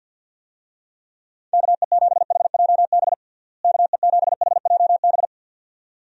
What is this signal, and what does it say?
Morse code, heard as a keyed tone, about 51 wpm, 700 Hz. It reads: KE7HJB KE7HJB
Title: USA Call Signs - 6 Characters - Repeated 2x - 50wpm